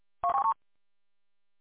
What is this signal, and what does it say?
There is no code.